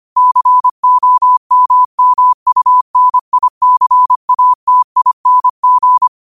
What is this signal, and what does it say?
Morse code, heard as a keyed tone, about 25 wpm, 1000 Hz. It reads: COMMUNICATING
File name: qotdfiles / 25wpm